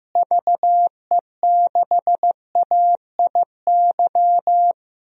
Morse code, heard as a keyed tone, about 15 wpm, 700 Hz. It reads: VE6AIY
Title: DXCC-100 Call Signs - 6 Characters - Repeated 3x - 15wpm